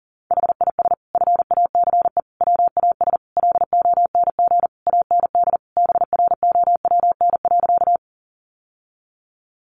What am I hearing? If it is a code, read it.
Morse code, heard as a keyed tone, about 40 wpm, 700 Hz. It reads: HIS FACE WAS LONG AND BROWN.
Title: Sentences from Top 1200 Words 40wpm